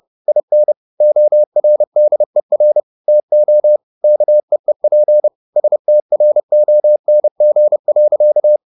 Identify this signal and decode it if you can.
Morse code, heard as a keyed tone, about 30 wpm, 600 Hz. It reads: IN ORDER TO KEEP STRONG.
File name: the old man and the sea / the old man and the sea 1 30wpm